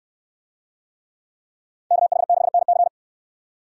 Morse code, heard as a keyed tone, about 66 wpm, 700 Hz. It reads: KH6NC